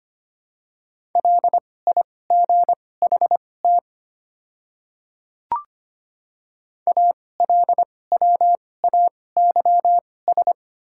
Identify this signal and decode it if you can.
Morse code, heard as a keyed tone, about 25 wpm, spaced as for 18 wpm, 700 Hz. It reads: LIGHT ALWAYS
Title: 2x Character Spacing - Top 300 Words - 25wpm